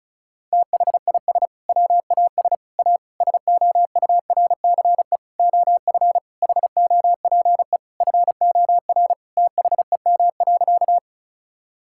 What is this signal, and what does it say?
Morse code, heard as a keyed tone, about 34 wpm, 700 Hz. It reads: THIS WAS A SOURCE OF HOPE FOR THEM.
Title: Sentences from Top 1000 Words - Review Part 5 - 35wpm